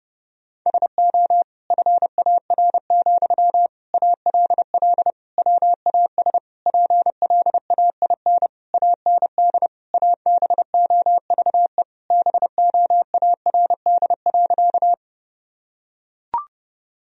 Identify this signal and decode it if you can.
Morse code, heard as a keyed tone, about 32 wpm, 700 Hz. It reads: SO FAR, ALL WAS PLAIN AND ABOVE BOARD.
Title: Sentences from Top 1000 Words - Review Part 4 - 30wpm